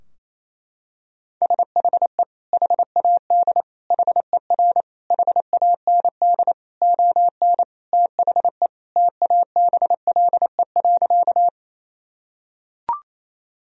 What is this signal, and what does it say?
Morse code, heard as a keyed tone, about 27 wpm, 690 Hz. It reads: SHE HAD HER HAND ON THE TABLE.
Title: Sentences from Top 1000 Words - Review Part 2 - 28wpm